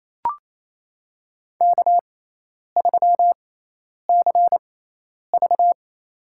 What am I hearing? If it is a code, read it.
Morse code, heard as a keyed tone, about 28 wpm, spaced as for 10 wpm, 700 Hz. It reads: K3CV
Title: USA Call Signs - 4 Characters - 6x Character Spacing - 28wpm